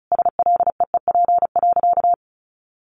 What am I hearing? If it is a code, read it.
Morse code, heard as a keyed tone, about 34 wpm, 700 Hz. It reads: SLEEP.
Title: Sentences from Top 1500 Words 35wpm